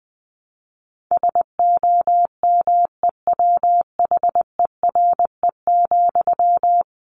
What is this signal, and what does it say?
Morse code, heard as a keyed tone, about 20 wpm, 700 Hz. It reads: SOMEWHERE,
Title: A Sound Of Thunder 20wpm